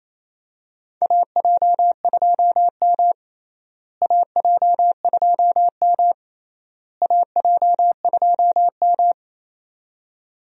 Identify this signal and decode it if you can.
Morse code, heard as a keyed tone, about 28 wpm, 700 Hz. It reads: AJ2M AJ2M AJ2M
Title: USA Call Signs - 4 Characters - Repeated 3x - 28wpm